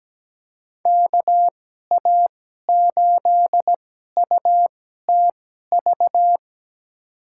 Morse code, heard as a keyed tone, about 17 wpm, spaced as for 12 wpm, 700 Hz. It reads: KA8UTV
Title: USA Call Signs - 6 Characters - 2x Character Spacing - 17wpm